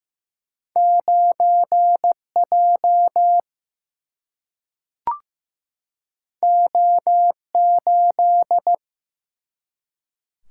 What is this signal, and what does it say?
Morse code, heard as a keyed tone, about 15 wpm, 700 Hz. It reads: 9 J O 8
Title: Single-Letter Single-Number 15wpm